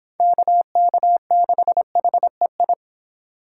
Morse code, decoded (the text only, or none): KK6HEI